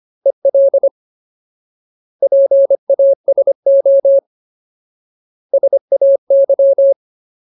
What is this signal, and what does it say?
Morse code, heard as a keyed tone, about 25 wpm, 550 Hz. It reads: EL PASO SAY
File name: Ultimate Word List / 25WPM Ultimate